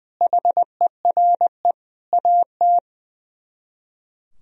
Morse code, heard as a keyed tone, about 21 wpm, 700 Hz. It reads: HERE AT
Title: Speed-Racing - Sets of 5 Words 20wpm